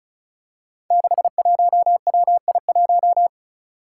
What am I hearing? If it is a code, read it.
Morse code, heard as a keyed tone, about 36 wpm, 700 Hz. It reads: 61WI1
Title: Five Challenging Letter-Number Combinations 35wpm